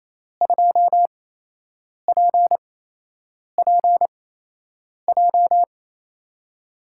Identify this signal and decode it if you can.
Morse code, heard as a keyed tone, about 28 wpm, spaced as for 8 wpm, 700 Hz. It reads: 2PPJ
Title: USA Call Signs - 5 Characters - 8x Character Spacing - 28wpm